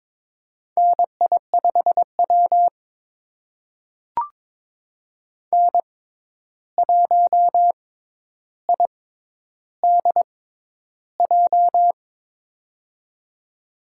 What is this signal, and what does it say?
Morse code, heard as a keyed tone, about 23 wpm, spaced as for 11 wpm, 700 Hz. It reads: NI5W N1IDJ
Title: USA Call Signs - 6x Character Spacing - 22wpm